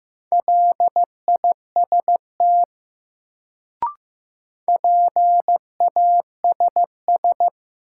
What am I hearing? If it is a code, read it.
Morse code, heard as a keyed tone, about 15 wpm, 700 Hz. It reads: LIST PASS
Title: Sentences from Top 400 Words - New Word Drill - 15wpm